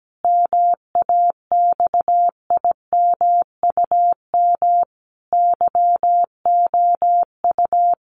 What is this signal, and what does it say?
Morse code, heard as a keyed tone, about 17 wpm, 700 Hz. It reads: MAXIMUM YOU
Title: Sentences from Top 1300 Words 17wpm